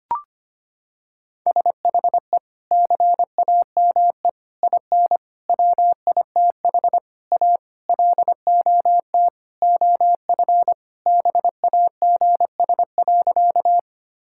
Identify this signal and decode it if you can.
Morse code, heard as a keyed tone, about 25 wpm, 700 Hz. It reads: SHE CAME IN WITH A LOT OF BAGS.
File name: practice / 25wpm / Sentences with 4-Word Phrases - 25wpm